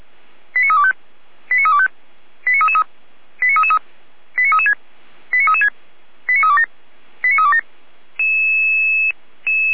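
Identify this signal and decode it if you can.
Five-tone selective call (ZVEI), heard as quick runs of five tones, tones of about 70 ms. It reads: 89325 89325 89333 89333 89336 89336 89327 89327
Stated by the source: Five-tone ZVEI selective call received by SDR.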